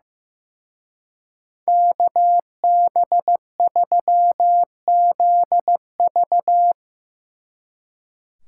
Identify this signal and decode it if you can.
Morse code, heard as a keyed tone, about 17 wpm, 700 Hz. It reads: KB3ZV
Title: Speed Racing - USA 2x2 Call Signs 15wpm